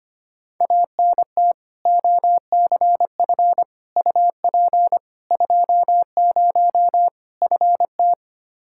Morse code, heard as a keyed tone, about 25 wpm, 700 Hz. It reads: ANT OCF UP 20 FT
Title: Practice QSO - Exchange 2 - Info - 25wpm